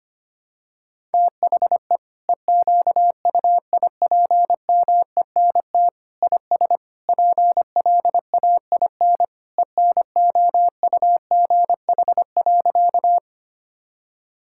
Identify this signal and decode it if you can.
Morse code, heard as a keyed tone, about 25 wpm, 700 Hz. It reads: THE EQUIPMENT IS PLAIN ENOUGH.
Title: Sentences from Top 1000 Words - Review Part 5 - 25wpm